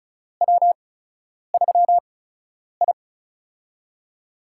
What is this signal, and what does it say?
Morse code, heard as a keyed tone, about 34 wpm, spaced as for 10 wpm, 700 Hz. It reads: W3I
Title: USA Call Signs - 4 Characters - 8x Character Spacing - 35wpm